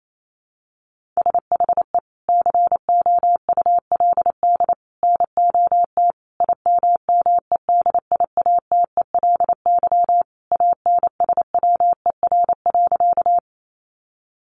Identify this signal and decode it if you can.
Morse code, heard as a keyed tone, about 28 wpm, 700 Hz. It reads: SHE COULD NOT IMMEDIATELY ANSWER.
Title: Sentences from Top 1100 Words 28wpm